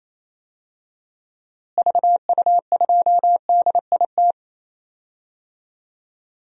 Morse code, heard as a keyed tone, about 28 wpm, 700 Hz. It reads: VU2DIT
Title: DXCC-100 Call Signs - 6 Characters - Repeated 3x - 28wpm